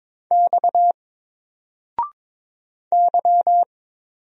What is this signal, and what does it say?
Morse code, heard as a keyed tone, about 22 wpm, 700 Hz. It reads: X Y
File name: Single-Letter-Number-22wpm